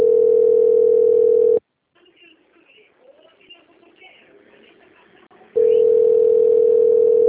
Ringback tone (North America), 440 + 480 Hz.